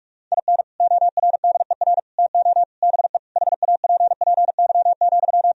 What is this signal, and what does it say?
Morse code, heard as a keyed tone, about 43 wpm, 700 Hz. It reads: IN ORDER TO BE HAPPY,
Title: Sentences from Top 1000 Words - Review Part 2 - 45wpm